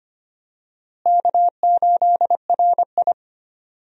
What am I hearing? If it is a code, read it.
Morse code, heard as a keyed tone, about 25 wpm, 700 Hz. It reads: K8RI
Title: USA Call Signs - 4 Characters - Repeated 3x - 25wpm